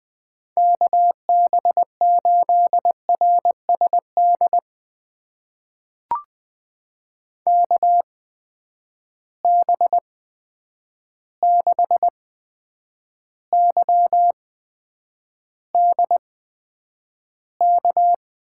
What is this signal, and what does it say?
Morse code, heard as a keyed tone, about 20 wpm, spaced as for 9 wpm, 700 Hz. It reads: KB8RSD KB6YDK
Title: USA Call Signs - 6 Characters - 8x Character Spacing - 20wpm